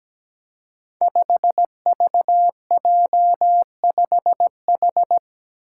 Morse code, heard as a keyed tone, about 17 wpm, 700 Hz. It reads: 5VJ5H
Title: Five Challenging Letter-Number Combinations 17wpm